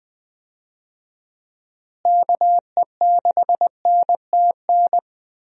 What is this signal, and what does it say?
Morse code, heard as a keyed tone, about 20 wpm, 700 Hz. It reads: KE6NTN
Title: USA Call Signs - 6 Characters - Repeated 3x - 20wpm